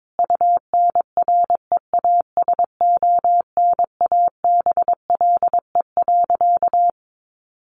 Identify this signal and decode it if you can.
Morse code, heard as a keyed tone, about 22 wpm, 700 Hz. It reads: UNREASONABLE.
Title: A Sound Of Thunder 22wpm